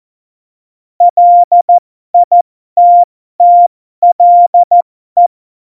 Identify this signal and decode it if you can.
Morse code, heard as a keyed tone, about 14 wpm, spaced as for 12 wpm, 700 Hz. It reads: LITTLE
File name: Morsekurs / Witze